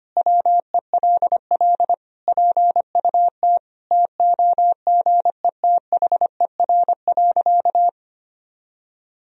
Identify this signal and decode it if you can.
Morse code, heard as a keyed tone, about 25 wpm, 700 Hz. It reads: WELL PUT TOGETHER.
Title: Sentences from Top 1000 Words - Review Part 3 - 25wpm